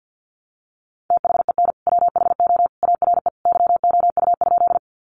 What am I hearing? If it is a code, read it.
Morse code, heard as a keyed tone, about 50 wpm, 700 Hz. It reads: THEN WHO ARE YOU?
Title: Sentences from Top 100 Words 50wpm